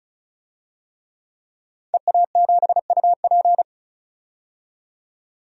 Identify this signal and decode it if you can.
Morse code, heard as a keyed tone, about 34 wpm, 700 Hz. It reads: EA7UP
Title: DXCC-100 Call Signs - 5 Characters - 35wpm